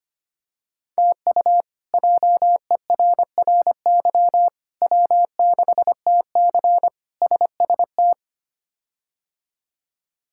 Morse code, heard as a keyed tone, about 25 wpm, 700 Hz. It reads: TU JERRY W6TC SST
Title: K1USN SST QSO Practice - 25wpm